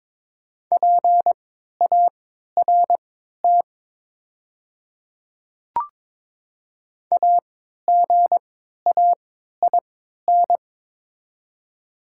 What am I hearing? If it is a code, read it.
Morse code, heard as a keyed tone, about 22 wpm, spaced as for 12 wpm, 700 Hz. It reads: PART AGAIN
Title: 3x Character Spacing - Top 300 Words - 22wpm